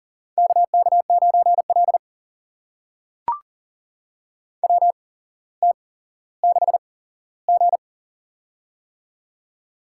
Morse code, heard as a keyed tone, about 44 wpm, spaced as for 17 wpm, 700 Hz. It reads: KK9L WT6G
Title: USA Call Signs - 4 Characters - 8x Character Spacing - 40wpm